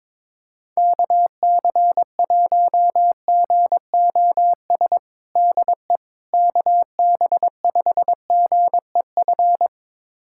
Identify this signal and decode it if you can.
Morse code, heard as a keyed tone, about 22 wpm, 700 Hz. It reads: KC1GOS DE KB5GEF
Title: USA Callsign DE Callsign - 6 Characters - 22wpm